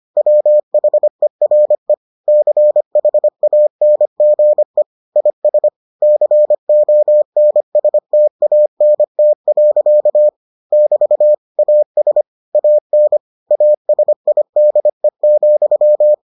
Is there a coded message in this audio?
Yes: Morse code from 0.2 to 16.1 s, heard as a keyed tone, about 25 wpm, 600 Hz. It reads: WHERE CHANGE IS CONSTANT. <BT> AS AN ASIDE,